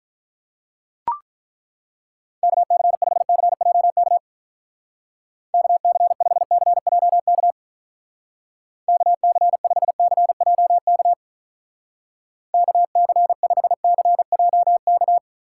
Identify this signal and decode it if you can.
Morse code, heard as a keyed tone, about 43 wpm, 700 Hz. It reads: KC5CJK KC5CJK KC5CJK KC5CJK
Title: Speed Racing - USA 2x3 Call Signs 35wpm